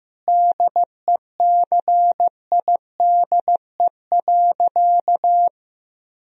Morse code, heard as a keyed tone, about 15 wpm, 700 Hz. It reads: DECIDE.
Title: Sentences from Top 1000 Words - Review Part 3 - 15wpm